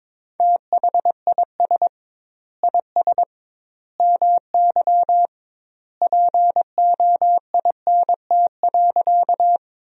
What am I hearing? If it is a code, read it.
Morse code, heard as a keyed tone, about 21 wpm, 700 Hz. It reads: THIS IS MY POINT.
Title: 2x Word-Spacing - Sentences from Top 200 Words - 22wpm